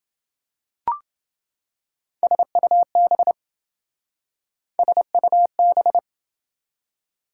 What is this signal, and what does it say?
Morse code, heard as a keyed tone, about 28 wpm, 700 Hz. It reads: SUB SUB